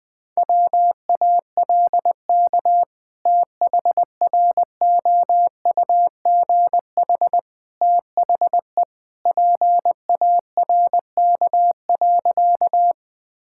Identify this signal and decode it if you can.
Morse code, heard as a keyed tone, about 20 wpm, 700 Hz. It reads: WALK THROUGH THE PARK.
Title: Sentences with 2-Word Phrases - 20wpm